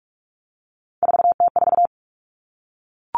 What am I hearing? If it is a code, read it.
Morse code, heard as a keyed tone, about 44 wpm, 700 Hz. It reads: 4T4
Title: Common Cut Numbers - 3-Digit 45wpm